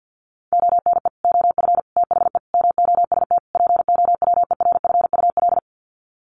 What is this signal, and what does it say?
Morse code, heard as a keyed tone, about 48 wpm, 700 Hz. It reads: ONE OF THE MOST POWERFUL
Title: Sets of 5 Words Encore 50wpm